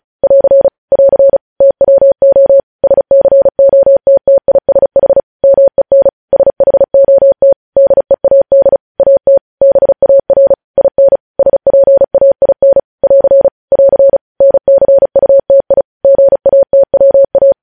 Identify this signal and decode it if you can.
Morse code, heard as a keyed tone, about 36 wpm, 550 Hz. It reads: <AR> <AR> TWO SCOTTISH MEN SHOT DEAD AT BAR IN SPAIN <AR> <AR> NCUTI GATWA
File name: podcast / News headlines-1x-35x35